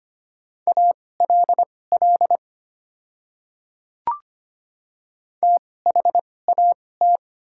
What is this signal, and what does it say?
Morse code, heard as a keyed tone, about 25 wpm, spaced as for 18 wpm, 690 Hz. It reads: ALL THAT
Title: 2x Character Spacing - Top 100 Words - 25wpm